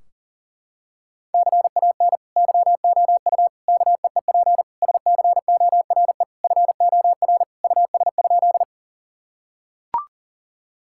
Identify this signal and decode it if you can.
Morse code, heard as a keyed tone, about 40 wpm, 700 Hz. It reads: CAN YOU KEEP SCORE FOR US?
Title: Sentences from Top 1600 Words 40wpm